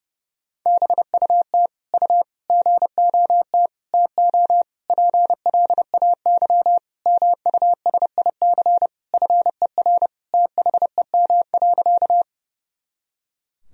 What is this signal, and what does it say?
Morse code, heard as a keyed tone, about 32 wpm, 700 Hz. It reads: BUT U GOT TO PLAY MUSIC FER THEM.
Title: Sentences from Top 300 Words and Abbreviations 30wpm